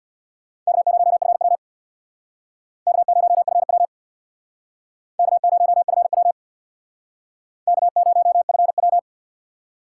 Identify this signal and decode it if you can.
Morse code, heard as a keyed tone, about 65 wpm, 700 Hz. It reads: K0FW K0FW K0FW K0FW